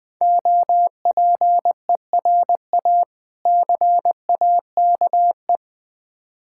Morse code, heard as a keyed tone, about 21 wpm, 700 Hz. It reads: OPERA CAKE
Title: Difficult Words with Hints - 20wpm